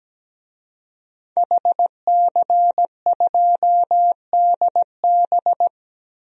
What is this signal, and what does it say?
Morse code, heard as a keyed tone, about 17 wpm, 700 Hz. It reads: HC2DB
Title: DXCC-100 Call Signs - 5 Characters - Repeated 2x - 17wpm